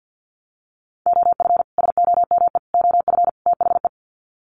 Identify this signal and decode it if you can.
Morse code, heard as a keyed tone, about 48 wpm, 700 Hz. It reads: OF SOME OF THE
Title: Sets of 4 Words 50wpm